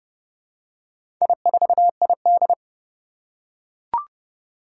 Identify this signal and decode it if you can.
Morse code, heard as a keyed tone, about 30 wpm, 700 Hz. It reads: I4ID
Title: Four Challenging Letter-Number Combinations 30wpm